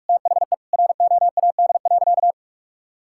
Morse code, heard as a keyed tone, about 41 wpm, 700 Hz. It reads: THE ROAD.
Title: Sentences from Top 1600 Words 45wpm